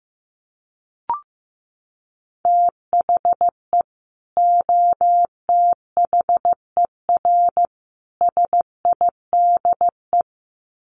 Morse code, heard as a keyed tone, about 15 wpm, 700 Hz. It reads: THE OTHER SIDE
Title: Sets of 3 Words 15wpm